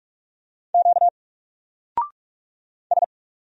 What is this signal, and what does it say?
Morse code, heard as a keyed tone, about 46 wpm, 700 Hz. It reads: Q S
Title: Single Character (Letter:Number:Punctuation:Prosign) Alt 45wpm